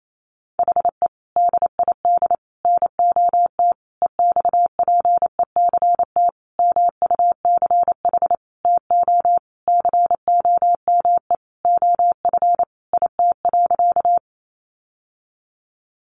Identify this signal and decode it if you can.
Morse code, heard as a keyed tone, about 28 wpm, 700 Hz. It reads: HE DID NOT EXPECT MUCH TO COME OF IT.
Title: Sentences from Top 1100 Words 28wpm